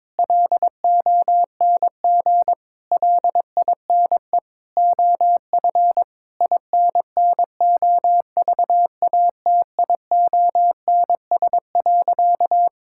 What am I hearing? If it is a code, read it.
Morse code, heard as a keyed tone, about 22 wpm, 700 Hz. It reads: LONG LINE OF INNOVATIONS.
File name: practice / 22wpm / Sentences with 8-Word Phrases - 22wpm